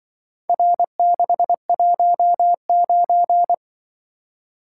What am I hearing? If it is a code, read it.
Morse code, heard as a keyed tone, about 24 wpm, 700 Hz. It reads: R619
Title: Speed-Racing - Four Letter-Number Combinations - 20wpm